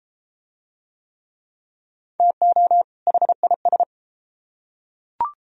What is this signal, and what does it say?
Morse code, heard as a keyed tone, about 31 wpm, 700 Hz. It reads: TO HIS